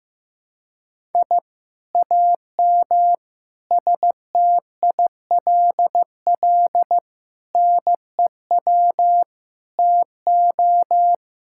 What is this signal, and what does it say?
Morse code, heard as a keyed tone, about 15 wpm, 700 Hz. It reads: I AM STILL NEW TO
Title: Sentences from Top 1000 Words - Review Part 6 - 15wpm